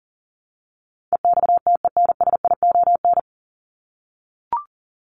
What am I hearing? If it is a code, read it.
Morse code, heard as a keyed tone, about 40 wpm, 700 Hz. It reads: EXTENSION
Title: Top 1501-1750 Words 40wpm